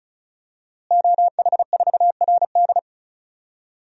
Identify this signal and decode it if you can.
Morse code, heard as a keyed tone, about 36 wpm, 700 Hz. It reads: OH4RD
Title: DXCC-100 Call Signs - 5 Characters - Repeated 3x - 35wpm